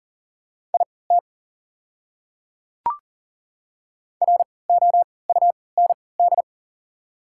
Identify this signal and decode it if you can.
Morse code, heard as a keyed tone, about 40 wpm, spaced as for 23 wpm, 700 Hz. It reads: IT ROUND
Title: 3x Character Spacing - Top 200 Words - 40wpm